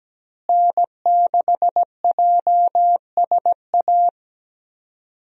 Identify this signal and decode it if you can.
Morse code, heard as a keyed tone, about 17 wpm, 700 Hz. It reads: N6JSA
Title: USA Call Signs - 5 Characters - Repeated 3x - 17wpm